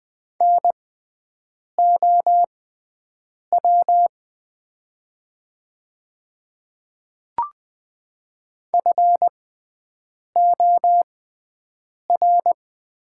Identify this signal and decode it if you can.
Morse code, heard as a keyed tone, about 20 wpm, spaced as for 7 wpm, 700 Hz. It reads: NOW FOR
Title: 6x Character Spacing - Top 100 Words - 20wpm